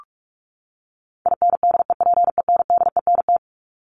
Morse code, heard as a keyed tone, about 44 wpm, 700 Hz. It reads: INDEPENDENT
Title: Top 751-1000 Words 45wpm